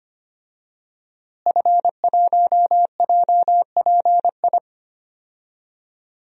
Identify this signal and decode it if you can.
Morse code, heard as a keyed tone, about 25 wpm, 700 Hz. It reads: F1JPI
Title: DXCC-100 Call Signs - 5 Characters - Repeated 2x - 25wpm